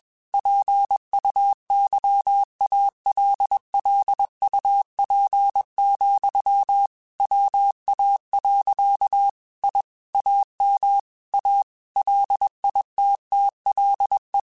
Morse code, heard as a keyed tone, about 21 wpm, 800 Hz. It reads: PUYALLUP, WA. I AM A LITTLE
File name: QSOs / QSOs-mp3s / WA7QOX 2@20